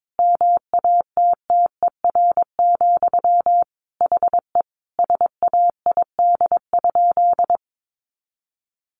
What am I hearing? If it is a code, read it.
Morse code, heard as a keyed tone, about 22 wpm, 700 Hz. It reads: MATTER, HE SAID?
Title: Sentences from Top 700 Words 22wpm